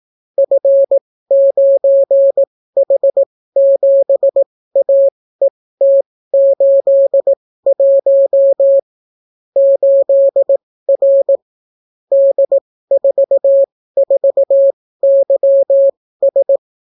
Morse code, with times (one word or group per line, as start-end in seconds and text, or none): 0.4-8.8 F9H7AET81
9.6-11.4 8R
12.1-16.6 D44YS